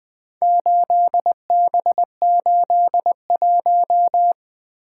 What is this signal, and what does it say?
Morse code, heard as a keyed tone, about 20 wpm, 700 Hz. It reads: 8B81
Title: Four Letter-Number Combinations - 20wpm